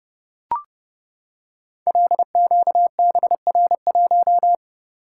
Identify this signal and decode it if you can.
Morse code, heard as a keyed tone, about 30 wpm, 700 Hz. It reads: LQBR1
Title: Code Groups - 5 Letter:Numbers - 30wpm